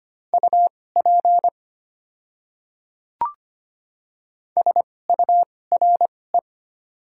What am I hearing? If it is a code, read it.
Morse code, heard as a keyed tone, about 25 wpm, spaced as for 18 wpm, 700 Hz. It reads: UP SURE
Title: 2x Character Spacing - Top 300 Words - 25wpm